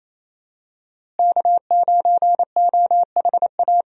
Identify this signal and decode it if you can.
Morse code, heard as a keyed tone, about 27 wpm, 700 Hz. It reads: K9OHA
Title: USA Call Signs - 5 Characters - Repeated 3x - 28wpm